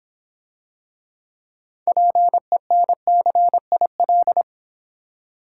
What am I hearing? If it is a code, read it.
Morse code, heard as a keyed tone, about 26 wpm, 700 Hz. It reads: PENCIL